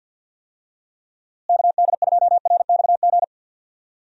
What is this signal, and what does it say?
Morse code, heard as a keyed tone, about 51 wpm, 700 Hz. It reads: KD2RXG
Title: USA Call Signs - 6 Characters - Repeated 2x - 50wpm